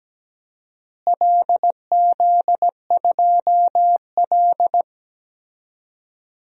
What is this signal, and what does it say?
Morse code, heard as a keyed tone, about 17 wpm, 700 Hz. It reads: LZ2L
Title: DXCC-100 Call Signs - 4 Characters - Repeated 3x - 17wpm